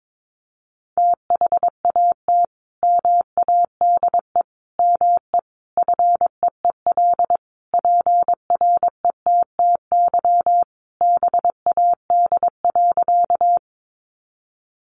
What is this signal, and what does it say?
Morse code, heard as a keyed tone, about 22 wpm, 700 Hz. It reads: THAT MADE ME FEEL PRETTY BAD.
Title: Sentences from Top 1000 Words 22wpm